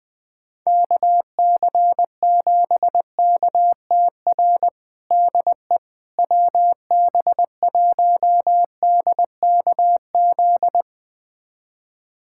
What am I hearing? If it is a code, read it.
Morse code, heard as a keyed tone, about 20 wpm, 700 Hz. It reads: KC7KTR DE WB1DKZ
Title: USA Callsign DE Callsign 20wpm